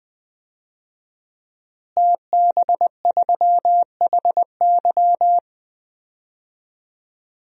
Morse code, heard as a keyed tone, about 20 wpm, 700 Hz. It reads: TB3HY
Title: DXCC-100 Call Signs - 5 Characters - Repeated 3x - 20wpm